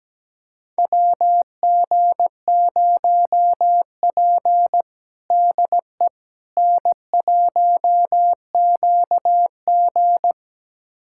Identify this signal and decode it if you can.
Morse code, heard as a keyed tone, about 17 wpm, 700 Hz. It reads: WG0P DE N1QG
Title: USA Callsign DE Callsign - 4 Characters - 17wpm